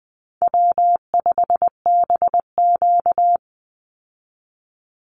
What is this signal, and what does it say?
Morse code, heard as a keyed tone, about 20 wpm, 700 Hz. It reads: W5BQ
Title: Easy Call Signs Repeated 2x - 20wpm